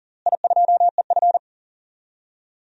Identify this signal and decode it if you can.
Morse code, heard as a keyed tone, about 40 wpm, 700 Hz. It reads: I2EF
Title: DXCC-100 Call Signs - 4 Characters - Repeated 3x - 40wpm